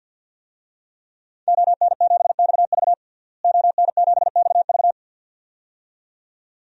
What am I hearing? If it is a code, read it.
Morse code, heard as a keyed tone, about 47 wpm, 700 Hz. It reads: ON7XV ON7XV
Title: DXCC-100 Call Signs - 5 Characters - Repeated 2x - 50wpm